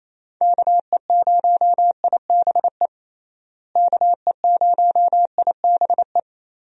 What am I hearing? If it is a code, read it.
Morse code, heard as a keyed tone, about 28 wpm, 700 Hz. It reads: KE0IBE KE0IBE
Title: USA Call Signs - 6 Characters - Repeated 3x - 28wpm